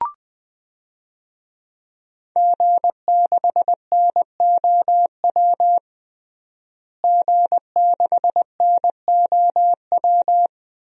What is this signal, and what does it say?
Morse code, heard as a keyed tone, about 20 wpm, 700 Hz. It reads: G6NOW G6NOW
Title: DXCC-100 Call Signs - 5 Characters - Repeated 3x - 20wpm